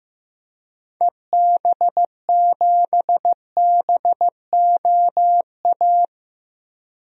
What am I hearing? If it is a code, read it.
Morse code, heard as a keyed tone, about 15 wpm, 700 Hz. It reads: EB7BOA
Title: DXCC-100 Call Signs - 6 Characters - Repeated 3x - 15wpm